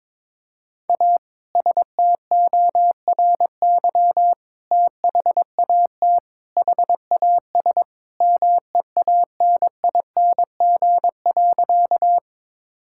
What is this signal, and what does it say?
Morse code, heard as a keyed tone, about 22 wpm, 700 Hz. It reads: A STORY THAT HAS MEANING.
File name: practice / 22wpm / Sentences with 2-Word Phrases - 22wpm